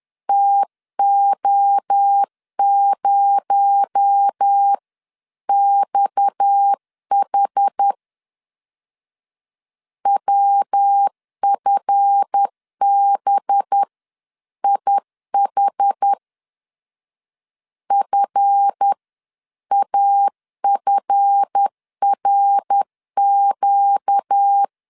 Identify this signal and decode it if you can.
Morse code, heard as a keyed tone, about 10 wpm, spaced as for 9 wpm, 800 Hz. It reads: TO0XH WFBIH FAFRQ
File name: LEZIONI / 7 GF9Z / 7 GF 10wpm 2514